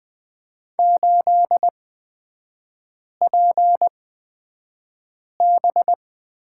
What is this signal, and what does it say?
Morse code, heard as a keyed tone, about 20 wpm, spaced as for 5 wpm, 690 Hz. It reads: 8PB
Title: 229 Lesson 37e - Letter x - Three Characters 20wpm